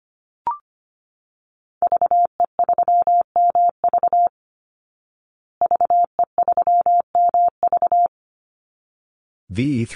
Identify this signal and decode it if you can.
Morse code, heard as a keyed tone, about 25 wpm, 700 Hz. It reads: VE3MV VE3MV
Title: Call Signs Repeated 3x - 25wpm